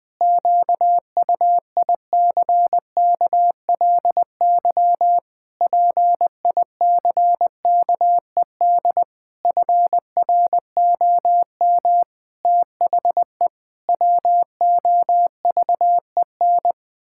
Morse code, heard as a keyed tone, about 21 wpm, 700 Hz. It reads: QUICKLY PICKED FROM THE WOVEN
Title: Pangram Challenge v2 - 20wpm